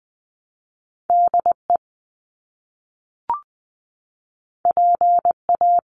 Morse code, heard as a keyed tone, about 20 wpm, 700 Hz. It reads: DE PA
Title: US States Repeated 3x 20wpm